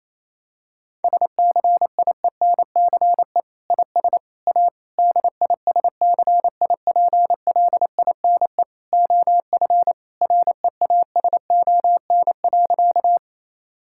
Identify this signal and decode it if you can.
Morse code, heard as a keyed tone, about 28 wpm, 700 Hz. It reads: SCIENCE IS A DISCIPLINE OF REASON.
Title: Sentences from Top 1000 Words - Review Part 6 - 28wpm